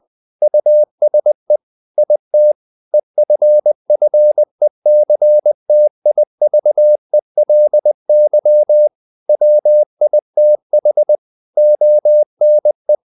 Morse code, heard as a keyed tone, about 20 wpm, 600 Hz. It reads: USE IT EFFECTIVELY WITH ONE
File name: the old man and the sea part 2 20wpm